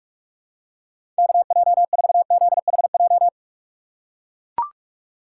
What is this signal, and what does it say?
Morse code, heard as a keyed tone, about 43 wpm, 700 Hz. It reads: KJ4ZHJ
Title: USA Call Signs - 6 Characters - 2x Character Spacing - 45wpm